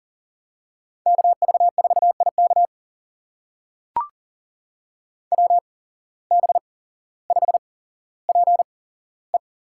Morse code, heard as a keyed tone, about 40 wpm, spaced as for 17 wpm, 700 Hz. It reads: KV4IK WB5PE
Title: USA Call Signs - 8x Character Spacing - 40wpm